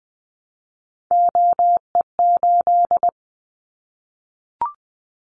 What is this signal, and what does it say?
Morse code, heard as a keyed tone, about 20 wpm, 700 Hz. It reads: OE8
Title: Callsign Prefixes for Top 100 DXCC - 20wpm